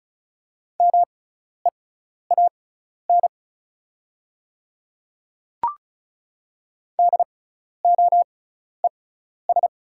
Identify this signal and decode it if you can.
Morse code, heard as a keyed tone, about 36 wpm, spaced as for 12 wpm, 700 Hz. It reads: MEAN DOES
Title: 6x Character Spacing - Top 200 Words - 35wpm